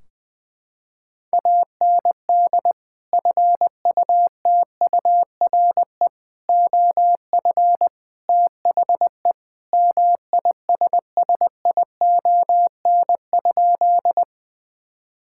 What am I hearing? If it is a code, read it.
Morse code, heard as a keyed tone, about 20 wpm, 700 Hz. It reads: AND FUTURE OF THE MISSION?
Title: Sentences from Top 1600 Words 20wpm